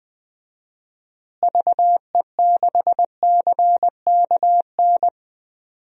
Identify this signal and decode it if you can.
Morse code, heard as a keyed tone, about 21 wpm, 700 Hz. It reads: VE6CKN
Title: DXCC-100 Call Signs - 6 Characters - Repeated 3x - 20wpm